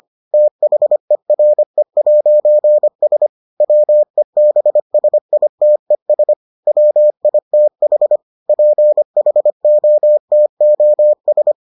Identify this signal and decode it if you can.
Morse code, heard as a keyed tone, about 25 wpm, 600 Hz. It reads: THERE'S WEBSITES WITH PHOTOS